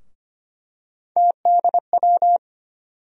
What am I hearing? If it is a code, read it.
Morse code, heard as a keyed tone, about 25 wpm, 700 Hz. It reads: TDW
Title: USA Call Sign Postfixes - 3 Letters 25wpm